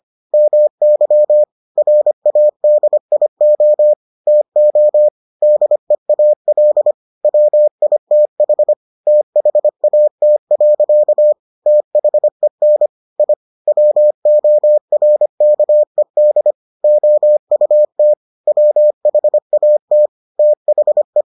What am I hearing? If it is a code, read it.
Morse code, heard as a keyed tone, about 25 wpm, 600 Hz. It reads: MY RADIO TO DEAL WITH THAT. THEN I WORKED OUT WHAT THE